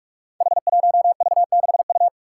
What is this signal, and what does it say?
Morse code, heard as a keyed tone, about 47 wpm, 700 Hz. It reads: H1V6U
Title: Five Challenging Letter-Number Combinations 45wpm